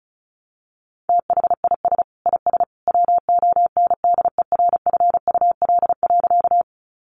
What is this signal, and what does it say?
Morse code, heard as a keyed tone, about 34 wpm, 700 Hz. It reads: THIS IS WONDERFUL.
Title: Sentences from Top 1100 Words 35wpm